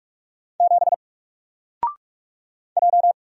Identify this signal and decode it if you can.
Morse code, heard as a keyed tone, about 45 wpm, 700 Hz. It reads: 7 J
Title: Single Letter-Number Alt 45wpm